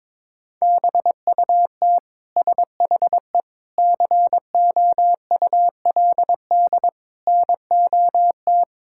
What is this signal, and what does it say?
Morse code, heard as a keyed tone, about 22 wpm, 700 Hz. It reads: BUT SHE COULD NOT
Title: Sentences from Top 1000 Words - Review Part 6 - 22wpm